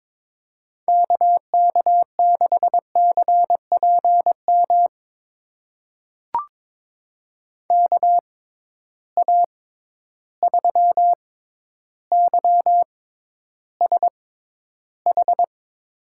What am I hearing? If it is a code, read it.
Morse code, heard as a keyed tone, about 22 wpm, spaced as for 11 wpm, 700 Hz. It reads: KK6CPM KA3YSH
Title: USA Call Signs - 6 Characters - 6x Character Spacing - 22wpm